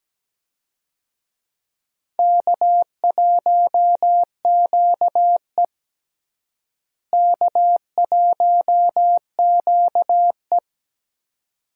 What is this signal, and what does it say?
Morse code, heard as a keyed tone, about 17 wpm, 700 Hz. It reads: K1QE K1QE
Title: USA Call Signs - 4 Characters - Repeated 2x - 17wpm